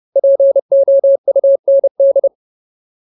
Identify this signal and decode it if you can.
Morse code, heard as a keyed tone, about 30 wpm, 550 Hz. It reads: POUND
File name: Ultimate Word List / 30WPM Ultimate